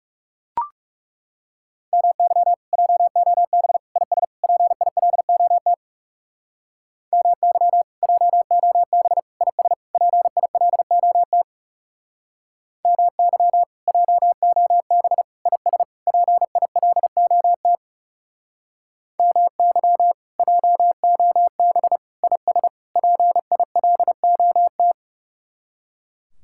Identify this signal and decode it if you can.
Morse code, heard as a keyed tone, about 35 wpm, 700 Hz. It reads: MY JOB IS PILOT MY JOB IS PILOT MY JOB IS PILOT MY JOB IS PILOT
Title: Speed-Racing Common QSO Phrases - 30wpm